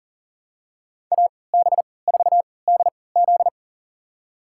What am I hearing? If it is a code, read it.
Morse code, heard as a keyed tone, about 40 wpm, spaced as for 23 wpm, 700 Hz. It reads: AB4DZ
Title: USA Call Signs - 5 Characters - 3x Character Spacing - 40wpm